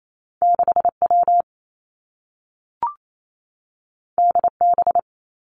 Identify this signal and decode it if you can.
Morse code, heard as a keyed tone, about 28 wpm, 700 Hz. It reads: 6 W D B
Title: Two Challenging Letter-Number Combinations 28wpm